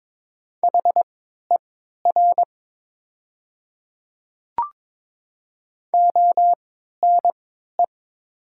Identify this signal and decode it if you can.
Morse code, heard as a keyed tone, about 22 wpm, spaced as for 12 wpm, 700 Hz. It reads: HER ONE
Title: 3x Character Spacing - Top 100 Words - 22wpm